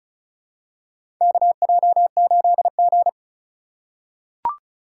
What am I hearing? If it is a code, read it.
Morse code, heard as a keyed tone, about 36 wpm, 700 Hz. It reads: KJ8G
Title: USA Call Signs - 2x Character Spacing - 35wpm